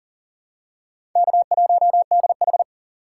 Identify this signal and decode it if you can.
Morse code, heard as a keyed tone, about 40 wpm, 700 Hz. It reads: K1DH
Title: USA Call Signs - 4 Characters - 6x Character Spacing - 40wpm